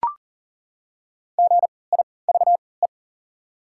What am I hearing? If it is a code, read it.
Morse code, heard as a keyed tone, about 37 wpm, spaced as for 22 wpm, 700 Hz. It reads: GIVE
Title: 3x Character Spacing - Top 200 Words - 40wpm